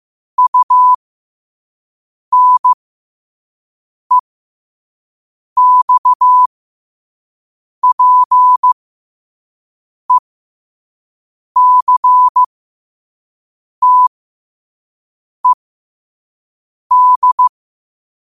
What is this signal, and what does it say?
Morse code, heard as a keyed tone, about 15 wpm, spaced as for 5 wpm, 1000 Hz. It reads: UNEXPECTED
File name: qotdfiles / 5wpm